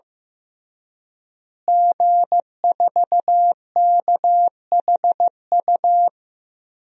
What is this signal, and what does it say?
Morse code, heard as a keyed tone, about 15 wpm, 700 Hz. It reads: G4KHU
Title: DXCC-100 Call Signs - 5 Characters - Repeated 3x - 15wpm